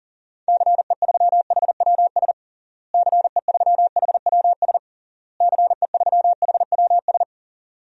Morse code, heard as a keyed tone, about 40 wpm, 700 Hz. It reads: CE3HWS CE3HWS CE3HWS
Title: DXCC-100 Call Signs - 6 Characters - Repeated 3x - 40wpm